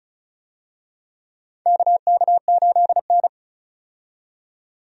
Morse code, heard as a keyed tone, about 36 wpm, 700 Hz. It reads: KK8N
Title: USA Call Signs - 4 Characters - 35wpm